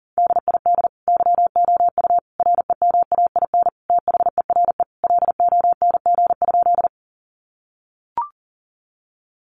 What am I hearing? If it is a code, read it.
Morse code, heard as a keyed tone, about 40 wpm, 700 Hz. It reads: DID YOU REMAIN THERE LONG?
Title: Sentences from Top 1200 Words 40wpm